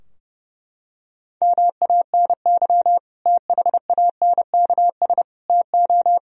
Morse code, heard as a keyed tone, about 30 wpm, 700 Hz. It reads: MANY THANKS TO
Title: Sentences from Top 1000 Words - Review Part 3 - 30wpm